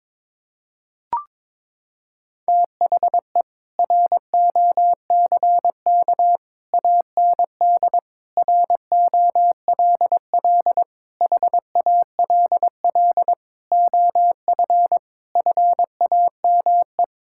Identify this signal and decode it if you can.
Morse code, heard as a keyed tone, about 22 wpm, 700 Hz. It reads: THE ROCK AND ROLL HALL OF FAME
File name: practice / 22wpm / Sets of 7 Words - 22wpm